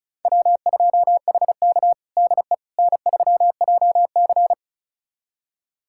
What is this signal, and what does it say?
Morse code, heard as a keyed tone, about 36 wpm, 700 Hz. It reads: W2HK DE N3JC
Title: USA Callsign DE Callsign - 4 Characters - 35wpm